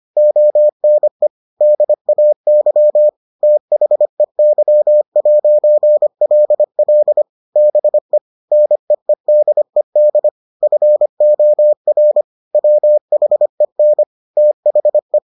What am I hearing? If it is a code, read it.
Morse code, heard as a keyed tone, about 25 wpm, 600 Hz. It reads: ONE DAY THEY'LL BE NEEDED FOR WHEN THE